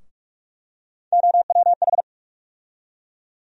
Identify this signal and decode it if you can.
Morse code, heard as a keyed tone, about 46 wpm, 700 Hz. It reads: OWH
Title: USA Call Sign Postfixes - 3 Letters 45wpm